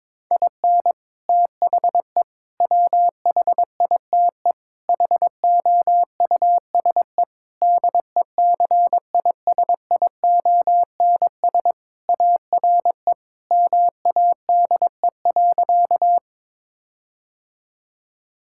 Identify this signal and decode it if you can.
Morse code, heard as a keyed tone, about 22 wpm, 700 Hz. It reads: IN THE WHITE HOUSE DECISIONS ARE MADE.
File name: practice / 22wpm / Sentences with 4-Word Phrases - 22wpm